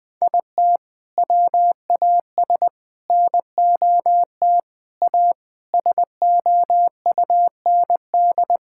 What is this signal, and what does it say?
Morse code, heard as a keyed tone, about 20 wpm, 700 Hz. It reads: IT WAS NOT A SOUND
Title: Sentences from Top 1000 Words - Review Part 6 - 20wpm